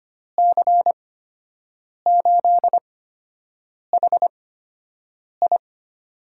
Morse code, heard as a keyed tone, about 25 wpm, spaced as for 7 wpm, 700 Hz. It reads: C8HI
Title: USA Call Signs - 5 Characters - 8x Character Spacing - 25wpm